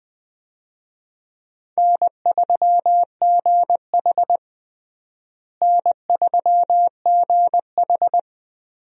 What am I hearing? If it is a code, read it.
Morse code, heard as a keyed tone, about 20 wpm, 700 Hz. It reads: N3GH N3GH
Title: USA Call Signs - 4 Characters - Repeated 3x - 20wpm